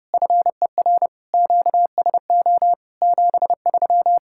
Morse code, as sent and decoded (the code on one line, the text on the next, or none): ..-. .-. / --.- ... --- / --... ...--
FER QSO 73